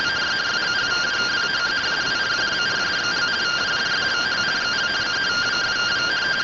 Two-tone teleprinter signal, about 170 Hz shift.